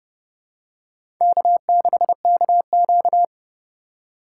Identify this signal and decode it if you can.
Morse code, heard as a keyed tone, about 30 wpm, 700 Hz. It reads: K6KQ
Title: USA Call Signs - 4 Characters - 2x Character Spacing - 30wpm